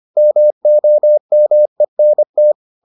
Morse code, heard as a keyed tone, about 25 wpm, 600 Hz. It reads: MOMENT